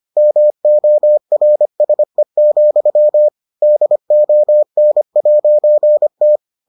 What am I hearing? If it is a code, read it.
Morse code, heard as a keyed tone, about 25 wpm, 600 Hz. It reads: MORSE, DON'T